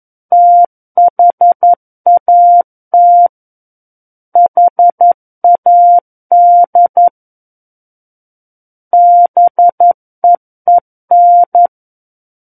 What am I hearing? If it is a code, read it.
Morse code, heard as a keyed tone, about 11 wpm, 700 Hz. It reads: THAT HAD BEEN
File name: 11 WPM